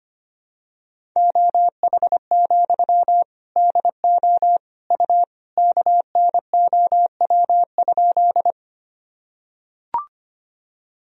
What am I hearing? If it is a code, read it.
Morse code, heard as a keyed tone, about 25 wpm, 700 Hz. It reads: OH, DO U KNOW?
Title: Sentences from Top 500 Words and Abbreviations 25wpm